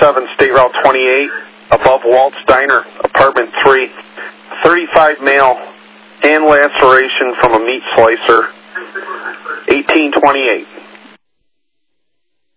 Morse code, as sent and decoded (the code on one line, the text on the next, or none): none